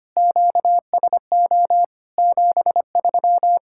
Morse code, heard as a keyed tone, about 25 wpm, 700 Hz. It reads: QSO 73
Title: Practice QSO - Exchange 4 - Ending - 25wpm